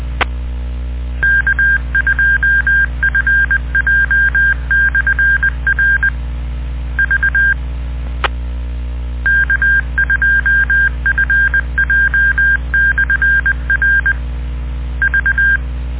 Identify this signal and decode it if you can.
Morse code, heard as a keyed tone, about 20 wpm, 1600 Hz. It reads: K2FJ/R V K2FJ/R V